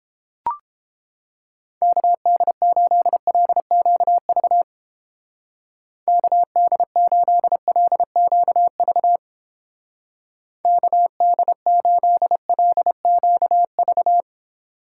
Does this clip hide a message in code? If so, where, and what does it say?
Yes: Morse code from 1.8 to 14.2 s, heard as a keyed tone, about 31 wpm, 700 Hz. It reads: KD8LQV KD8LQV KD8LQV